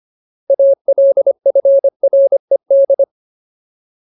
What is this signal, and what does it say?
Morse code, heard as a keyed tone, about 25 wpm, 550 Hz. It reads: ALFRED
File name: Ultimate Word List / 25WPM Ultimate